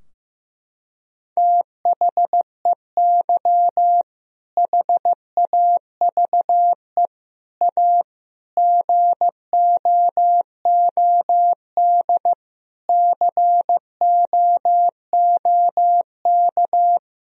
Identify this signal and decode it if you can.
Morse code, heard as a keyed tone, about 15 wpm, 700 Hz. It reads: THEY HAVE A GOOD COOK
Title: Sentences from Top 1000 Words - Review Part 4 - 15wpm